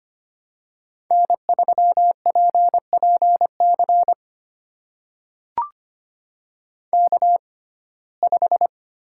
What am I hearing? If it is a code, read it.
Morse code, heard as a keyed tone, about 25 wpm, spaced as for 18 wpm, 700 Hz. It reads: N3PPC K5
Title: USA Call Signs - 5 Characters - 6x Character Spacing - 25wpm